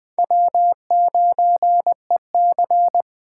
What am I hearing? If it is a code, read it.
Morse code, heard as a keyed tone, about 23 wpm, 700 Hz. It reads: W9EC
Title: Speed Racing - USA 1x2 Call Signs 20wpm